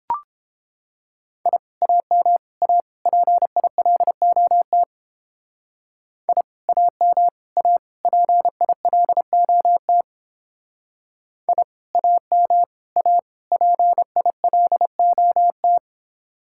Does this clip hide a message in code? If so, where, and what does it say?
Yes: Morse code from 1.5 to 15.8 s, heard as a keyed tone, about 31 wpm, 700 Hz. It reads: I AM A PILOT I AM A PILOT I AM A PILOT